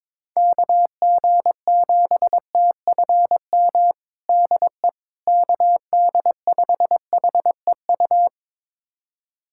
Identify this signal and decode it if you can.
Morse code, heard as a keyed tone, about 22 wpm, 700 Hz. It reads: KG7TFM DE KD5HEU
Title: USA Callsign DE Callsign - 6 Characters - 22wpm